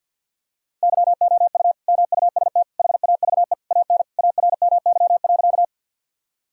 Morse code, heard as a keyed tone, about 50 wpm, 700 Hz. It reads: YOU MUST HAVE AN ARMY.
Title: Sentences from Top 1000 Words - Review Part 5 - 50wpm